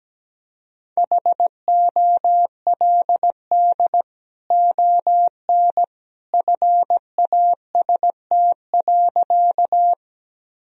Morse code, heard as a keyed tone, about 17 wpm, 700 Hz. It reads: HOLD ON FAST.
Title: Sentences from Top 1000 Words - Review Part 2 - 17wpm